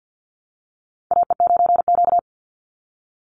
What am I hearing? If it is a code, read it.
Morse code, heard as a keyed tone, about 50 wpm, 700 Hz. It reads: AE9Q
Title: NA SOTA Chaser Callsigns 50wpm